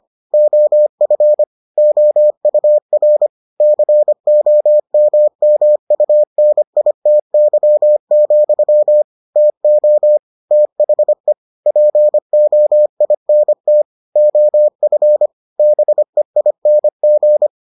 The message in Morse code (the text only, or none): OF OUR COMMUNITY, TO THE POINT OF BEING